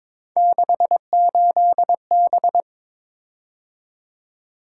Morse code, heard as a keyed tone, about 22 wpm, 700 Hz. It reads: 68B
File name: practice / 22wpm / Three Letter-Number Combinations - 22wpm